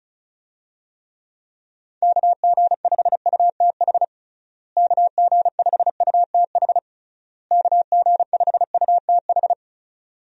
Morse code, heard as a keyed tone, about 36 wpm, 700 Hz. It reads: KG5UTH KG5UTH KG5UTH
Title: USA Call Signs - 6 Characters - Repeated 3x - 35wpm